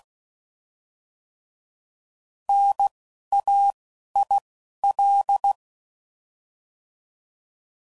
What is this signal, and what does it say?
Morse code, heard as a keyed tone, about 16 wpm, 800 Hz. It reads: N A I L
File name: Prefixes / By First Letter / Ns